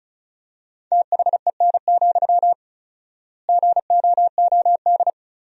Morse code, heard as a keyed tone, about 36 wpm, 700 Hz. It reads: THEN, GOOD
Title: 4x Word-Spacing - Sentences from Top 300 Words - 35wpm